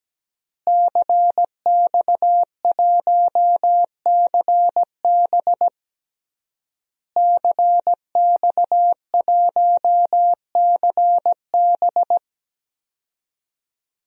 Morse code, heard as a keyed tone, about 17 wpm, 700 Hz. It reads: CX1CB CX1CB
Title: DXCC-100 Call Signs - 5 Characters - Repeated 3x - 17wpm